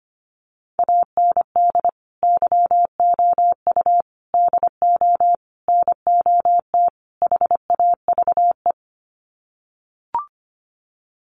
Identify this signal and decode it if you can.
Morse code, heard as a keyed tone, about 25 wpm, 690 Hz. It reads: AND YOU DO NOT HAVE
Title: Sets of 5 Words Encore 25wpm